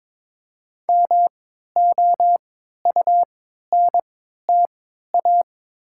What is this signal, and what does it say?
Morse code, heard as a keyed tone, about 22 wpm, spaced as for 12 wpm, 690 Hz. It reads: MOUNTA
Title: 3x Character Spacing - Top 500 Words - 22wpm